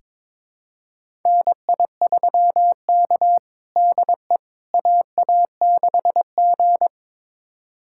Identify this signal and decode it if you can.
Morse code, heard as a keyed tone, about 22 wpm, 700 Hz. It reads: NI3K DE AA6G
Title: USA Callsign DE Callsign - 4 Characters - 22wpm